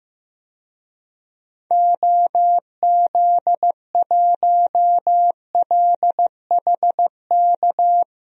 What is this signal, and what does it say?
Morse code, heard as a keyed tone, about 15 wpm, 700 Hz. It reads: OZ1LHK
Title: DXCC-100 Call Signs - 6 Characters - Repeated 2x - 15wpm